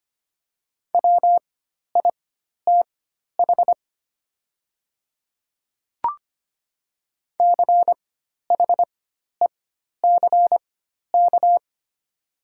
Morse code, heard as a keyed tone, about 25 wpm, spaced as for 12 wpm, 700 Hz. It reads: WITH CHECK
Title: 4x Character Spacing - Top 500 Words - 25wpm